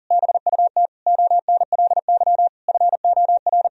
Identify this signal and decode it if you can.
Morse code, heard as a keyed tone, about 40 wpm, 700 Hz. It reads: BUT ONLY FOR
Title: Sentences from Top 1000 Words - Review Part 1 - 40wpm